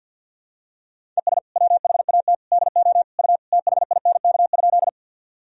Morse code, heard as a keyed tone, about 47 wpm, 700 Hz. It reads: ES WHAT DO U THINK?
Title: Sentences from Top 200 Words and Abbreviations 50wpm